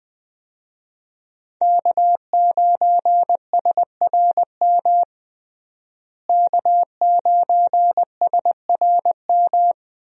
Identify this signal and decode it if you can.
Morse code, heard as a keyed tone, about 20 wpm, 700 Hz. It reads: K9SRM K9SRM
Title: USA Call Signs - 5 Characters - Repeated 2x - 20wpm